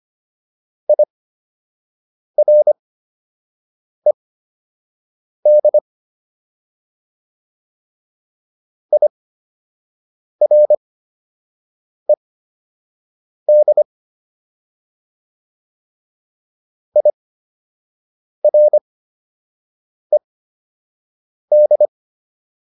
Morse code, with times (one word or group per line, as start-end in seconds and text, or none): 0.9-5.8 IRED
8.9-13.8 IRED
17.0-21.9 IRED